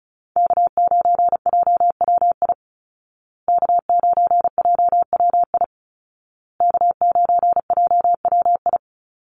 Morse code, heard as a keyed tone, about 36 wpm, 700 Hz. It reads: K9JWI K9JWI K9JWI
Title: Call Signs Repeated 2x - 35wpm